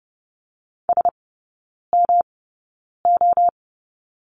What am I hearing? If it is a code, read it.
Morse code, heard as a keyed tone, about 30 wpm, spaced as for 9 wpm, 700 Hz. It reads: SMO
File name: Lesson 18e - Letter m - Three Characters 30wpm